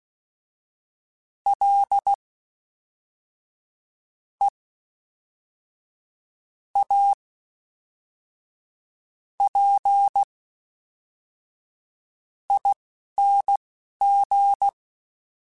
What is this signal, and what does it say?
Morse code, heard as a keyed tone, about 16 wpm, spaced as for 5 wpm, 800 Hz. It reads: LEAPING
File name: Special INGs / Special INGs-mp3s / Fast ing @10